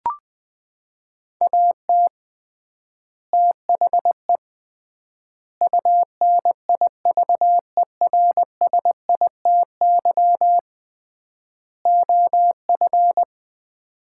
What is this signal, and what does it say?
Morse code, heard as a keyed tone, about 20 wpm, 700 Hz. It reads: AT THE UNIVERSITY OF
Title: 3x Word-Spacing - Sets of 4 Words - 20wpm